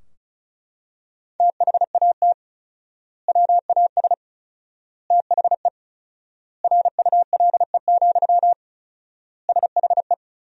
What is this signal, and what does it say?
Morse code, heard as a keyed tone, about 36 wpm, 700 Hz. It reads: THAT WAS THE RULE, SHE
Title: 4x Word-Spacing - Sentences from Top 400 Words - 35wpm